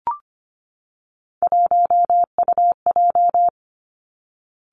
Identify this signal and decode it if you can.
Morse code, heard as a keyed tone, about 25 wpm, 700 Hz. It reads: 1UJ
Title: Three Challenging Letter-Number Combinations 25wpm